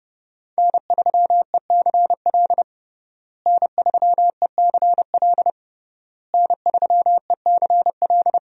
Morse code, heard as a keyed tone, about 30 wpm, 700 Hz. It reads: N3ECL N3ECL N3ECL
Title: USA Call Signs - 5 Characters - Repeated 3x - 30wpm